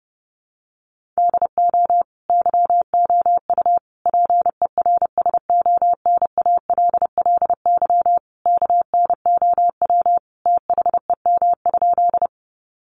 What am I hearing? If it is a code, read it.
Morse code, heard as a keyed tone, about 30 wpm, 700 Hz. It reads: DO YOU PERSONALLY KNOW THEM?
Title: Sentences from Top 1400 Words 30wpm